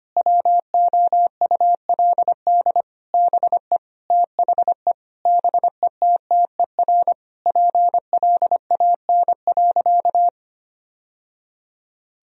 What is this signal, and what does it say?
Morse code, heard as a keyed tone, about 25 wpm, 700 Hz. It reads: WOULD BE THE BETTER PLAN.
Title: Sentences from Top 1000 Words - Review Part 2 - 25wpm